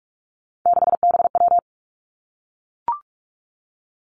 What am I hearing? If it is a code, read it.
Morse code, heard as a keyed tone, about 46 wpm, 700 Hz. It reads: 6BW
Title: Three Challenging Letter-Number Combinations 45wpm